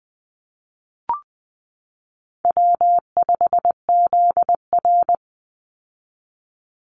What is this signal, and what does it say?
Morse code, heard as a keyed tone, about 20 wpm, 700 Hz. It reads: W5ZR
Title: Easy Call Signs Repeated 3x - 20wpm